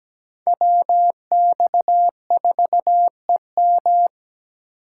Morse code, heard as a keyed tone, about 20 wpm, 700 Hz. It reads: WX4EM
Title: Speed Racing - USA 2x2 Call Signs 17wpm